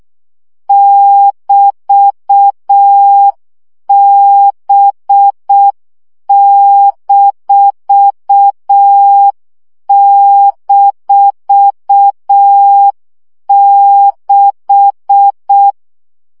Morse code, heard as a keyed tone, about 6 wpm, 800 Hz. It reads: <BT>B--6
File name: CW gyakorlas b 6 egyenlo minusz 5 baud